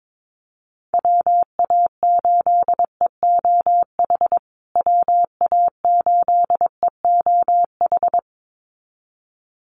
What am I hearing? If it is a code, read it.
Morse code, heard as a keyed tone, about 22 wpm, 700 Hz. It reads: WA8EOH WA8EOH
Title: USA Call Signs Repeated 2x 22wpm